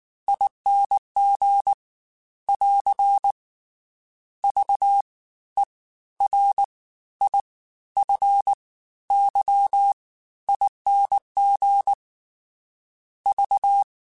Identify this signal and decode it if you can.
Morse code, heard as a keyed tone, about 19 wpm, spaced as for 13 wpm, 800 Hz. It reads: ING <AR> VERIFYING V
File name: Special INGs-mp3s / Special INGs-mp3s / Fast ing @18